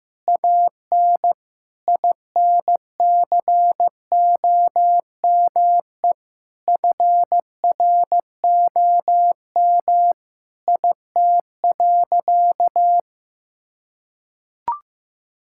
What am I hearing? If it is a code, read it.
Morse code, heard as a keyed tone, about 15 wpm, 700 Hz. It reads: AN INCOME FROM IT.
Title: Sentences from Top 1000 Words - Review Part 5 - 15wpm